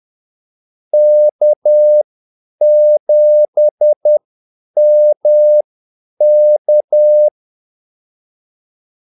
Morse code, heard as a keyed tone, about 10 wpm, spaced as for 8 wpm, 600 Hz. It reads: K7MK
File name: Callsigns-10-8wpm